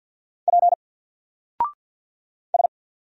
Morse code, heard as a keyed tone, about 48 wpm, 700 Hz. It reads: P S
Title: Single Character (Letter:Number:Punctuation:Prosign) 50wpm